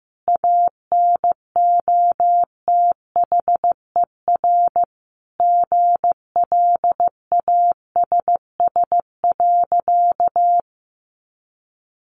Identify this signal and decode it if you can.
Morse code, heard as a keyed tone, about 14 wpm, 700 Hz. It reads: ANOTHER GLASS.
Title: Sentences from Top 1500 Words 15wpm